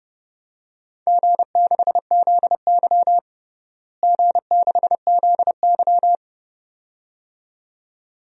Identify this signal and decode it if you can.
Morse code, heard as a keyed tone, about 30 wpm, 700 Hz. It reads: G6ZY G6ZY
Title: DXCC-100 Call Signs - 4 Characters - Repeated 2x - 30wpm